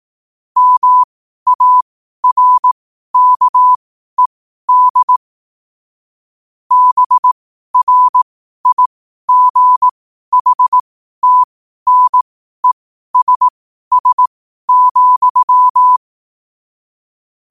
Morse code, heard as a keyed tone, about 18 wpm, spaced as for 12 wpm, 1000 Hz. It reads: MARKED BRIGHTNESS,
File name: qotdfiles / 12wpm